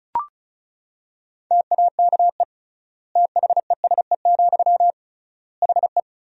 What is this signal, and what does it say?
Morse code, heard as a keyed tone, about 36 wpm, 700 Hz. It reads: TAKE THESE, HE
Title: 3x Word-Spacing - Sentences from Top 300 Words - 35wpm